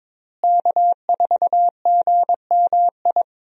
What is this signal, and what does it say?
Morse code, heard as a keyed tone, about 21 wpm, 700 Hz. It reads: K4GMI
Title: USA Call Signs - 5 Characters - Repeated 2x - 22wpm